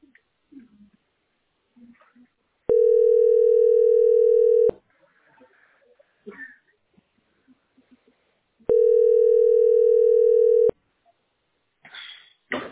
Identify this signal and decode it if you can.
Ringback tone (North America), 440 + 480 Hz.